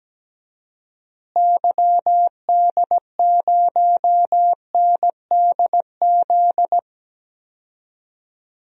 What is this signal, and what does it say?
Morse code, heard as a keyed tone, about 17 wpm, 700 Hz. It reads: YD0NDZ
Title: DXCC-100 Call Signs - 6 Characters - Repeated 3x - 17wpm